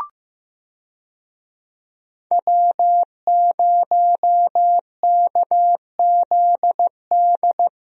Morse code, heard as a keyed tone, about 15 wpm, 700 Hz. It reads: W0KZD
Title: USA Call Signs - 5 Characters - Repeated 2x - 15wpm